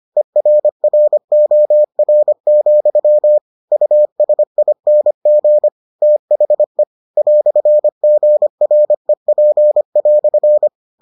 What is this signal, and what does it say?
Morse code, heard as a keyed tone, about 25 wpm, 600 Hz. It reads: ERROR, USING THE "GREP"